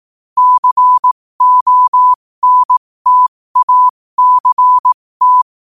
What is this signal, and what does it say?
Morse code, heard as a keyed tone, about 18 wpm, spaced as for 15 wpm, 1000 Hz. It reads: CONTACT
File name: qotdfiles / 15wpm